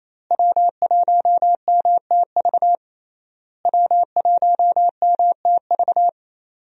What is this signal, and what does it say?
Morse code, heard as a keyed tone, about 28 wpm, 700 Hz. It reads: W1MTV W1MTV
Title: USA Call Signs - 5 Characters - Repeated 3x - 28wpm